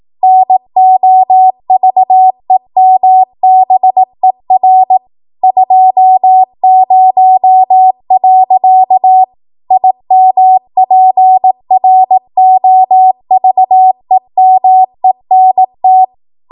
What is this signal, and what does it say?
Morse code, heard as a keyed tone, about 18 wpm, 750 Hz. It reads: NOVEMBER 20. IMPROVEMENT
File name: Archive / 18 WPM / 18WPM